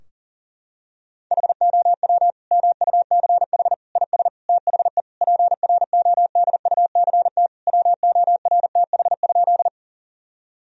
Morse code, heard as a keyed tone, about 40 wpm, 700 Hz. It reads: HOW MUCH IS THE PRODUCT WORTH?
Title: Sentences from Top 1000 Words - Review Part 2 - 40wpm